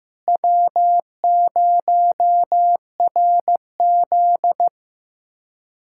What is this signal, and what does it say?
Morse code, heard as a keyed tone, about 15 wpm, 700 Hz. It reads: W0RZ
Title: USA Callsign DE Callsign - 4 Characters - 15wpm